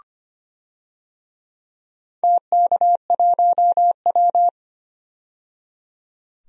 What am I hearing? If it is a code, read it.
Morse code, heard as a keyed tone, about 25 wpm, 700 Hz. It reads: TK1W
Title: DXCC-100 Call Signs - 4 Characters - 25wpm